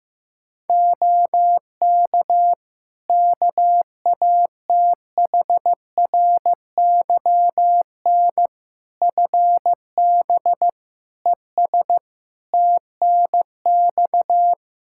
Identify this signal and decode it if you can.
Morse code, heard as a keyed tone, about 16 wpm, 700 Hz. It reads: OK KATHRYN FB ES TNX
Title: K1USN SST QSO Practice 15wpm